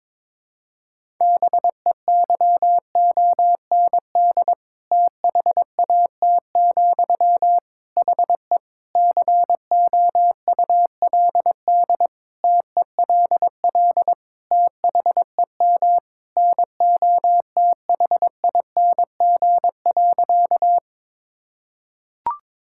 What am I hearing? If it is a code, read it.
Morse code, heard as a keyed tone, about 22 wpm, 700 Hz. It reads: BEYOND THAT, HE COULD TELL THEM NOTHING.
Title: Sentences from Top 1000 Words - Review Part 6 - 22wpm